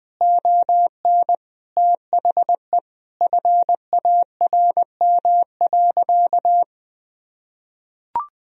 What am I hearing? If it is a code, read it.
Morse code, heard as a keyed tone, about 19 wpm, 700 Hz. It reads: ON THE FARM.
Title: Sentences from Top 1000 Words - Review Part 4 - 20wpm